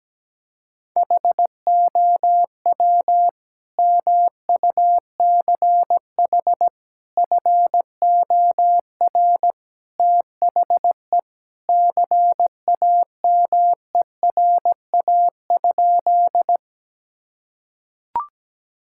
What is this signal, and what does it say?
Morse code, heard as a keyed tone, about 17 wpm, 700 Hz. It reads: HOW MUCH FOR THE CAMERA?
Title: Sentences from Top 1000 Words - Review Part 5 - 17wpm